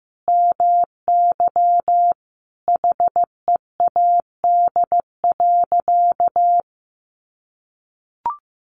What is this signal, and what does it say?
Morse code, heard as a keyed tone, about 15 wpm, 700 Hz. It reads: MY HEAD.
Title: Sentences from Top 1100 Words 15wpm